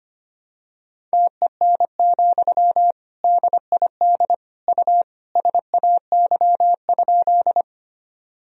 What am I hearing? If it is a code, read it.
Morse code, heard as a keyed tone, about 25 wpm, 700 Hz. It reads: TEN, DID U SAY?
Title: Sentences from Top 400 Words and Abbreviations 25wpm